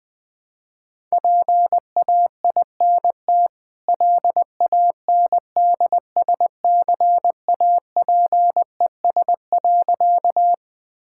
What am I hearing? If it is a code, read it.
Morse code, heard as a keyed tone, about 20 wpm, 700 Hz. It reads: PAINT LANDSCAPES.
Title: Easy Sentences with Hints - 20wpm